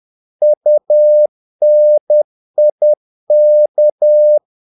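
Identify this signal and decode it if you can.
Morse code, heard as a keyed tone, about 10 wpm, 600 Hz. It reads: UNIK
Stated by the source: Morse practice of short Danish words at 10 wpm, from CW med Gnister.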